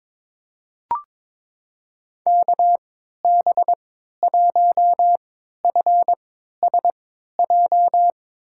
Morse code, heard as a keyed tone, about 22 wpm, spaced as for 12 wpm, 700 Hz. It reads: KB1FSJ
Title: USA Call Signs - 6 Characters - 3x Character Spacing - 22wpm